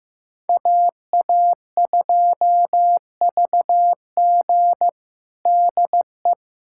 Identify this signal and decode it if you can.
Morse code, heard as a keyed tone, about 15 wpm, 700 Hz. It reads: AA2VG DE
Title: USA Callsign DE Callsign - 5 Characters - 15wpm